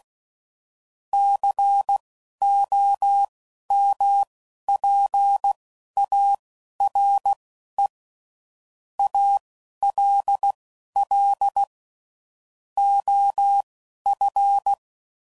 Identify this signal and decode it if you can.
Morse code, heard as a keyed tone, about 16 wpm, spaced as for 11 wpm, 800 Hz. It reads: COMPARE ALL OF
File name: CW Code Groups @10WPM / Technical Subjects / OR Gate